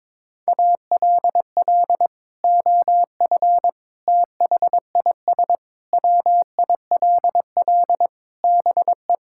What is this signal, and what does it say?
Morse code, heard as a keyed tone, about 22 wpm, 700 Hz. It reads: ALL OF THIS WILL BE
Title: Sentences from Top 1000 Words - Review Part 6 - 22wpm